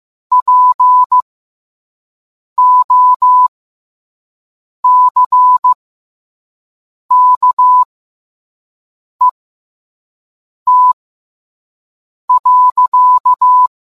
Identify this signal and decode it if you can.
Morse code, heard as a keyed tone, about 15 wpm, spaced as for 5 wpm, 1000 Hz. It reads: POCKET.